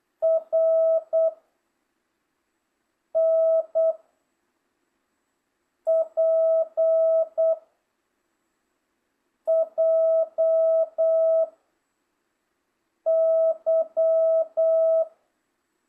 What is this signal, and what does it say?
Morse code, heard as a keyed tone, about 8 wpm, 1300 Hz, with noise. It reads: R N P J Y